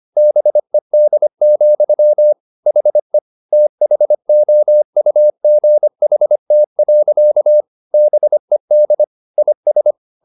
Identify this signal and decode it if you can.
Morse code, heard as a keyed tone, about 25 wpm, 600 Hz. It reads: BED, HE THOUGHT. BED IS